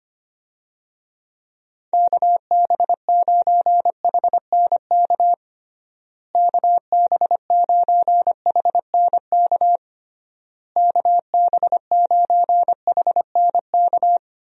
Morse code, heard as a keyed tone, about 25 wpm, 700 Hz. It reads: KB9HNK KB9HNK KB9HNK
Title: USA Call Signs - 6 Characters - Repeated 3x - 25wpm